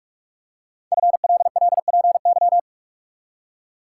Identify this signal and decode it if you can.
Morse code, heard as a keyed tone, about 44 wpm, 690 Hz. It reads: FLLPY
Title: Code Groups - 5 Letters - 45wpm